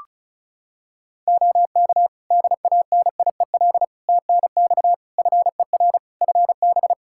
Morse code, heard as a keyed tone, about 34 wpm, 700 Hz. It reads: OK DANIEL TNX FER FB
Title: Practice QSO - Exchange 4 - Ending - 35wpm